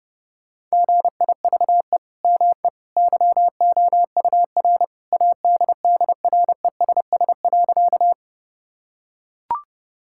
Morse code, heard as a keyed tone, about 30 wpm, 700 Hz. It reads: GIVE ME YOUR ADDRESS.
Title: Sentences from Top 1000 Words - Review Part 6 - 30wpm